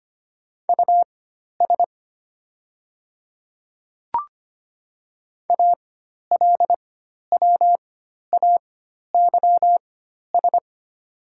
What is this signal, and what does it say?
Morse code, heard as a keyed tone, about 25 wpm, spaced as for 12 wpm, 700 Hz. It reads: US ALWAYS
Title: 4x Character Spacing - Top 300 Words - 25wpm